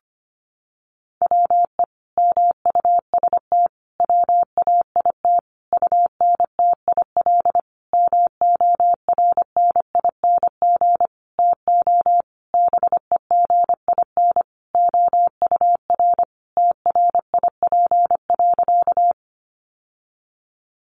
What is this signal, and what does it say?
Morse code, heard as a keyed tone, about 25 wpm, 700 Hz. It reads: WE MUST WAIT UNTIL MORNING TO BEGIN OUR TRIP.
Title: Sentences from Top 1300 Words 25wpm